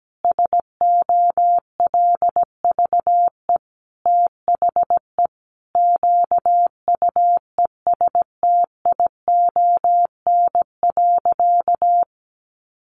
Morse code, heard as a keyed tone, about 17 wpm, 700 Hz. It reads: SOLVE THE QUESTION.
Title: Sentences from Top 1000 Words 17wpm